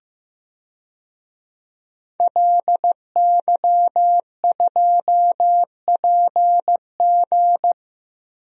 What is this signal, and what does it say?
Morse code, heard as a keyed tone, about 15 wpm, 700 Hz. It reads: LY2PG
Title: DXCC-100 Call Signs - 5 Characters - Repeated 2x - 15wpm